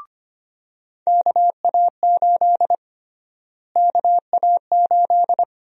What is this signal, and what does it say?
Morse code, heard as a keyed tone, about 25 wpm, 700 Hz. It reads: KA8 KA8
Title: USA Call Sign Prefixes - Repeated 2x - 25wpm